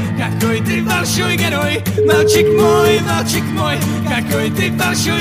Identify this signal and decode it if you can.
Ringback tone (Europe, 425 Hz), with noise.